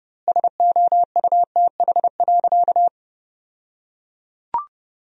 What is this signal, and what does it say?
Morse code, heard as a keyed tone, about 30 wpm, 700 Hz. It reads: SOUTH.
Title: 2x Word-Spacing - Sentences from Top 400 Words - 30wpm